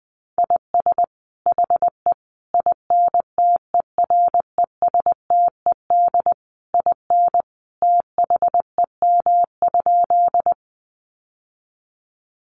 Morse code, heard as a keyed tone, about 20 wpm, 700 Hz. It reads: IS HE INTERESTED IN THEM?
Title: Sentences from Top 900 Words 20wpm